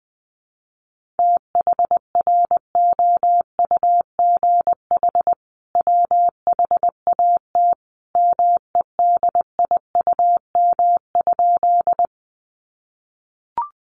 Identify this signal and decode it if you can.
Morse code, heard as a keyed tone, about 20 wpm, 700 Hz. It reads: THROUGH WHAT MEDIUM?
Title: Sentences from Top 900 Words 20wpm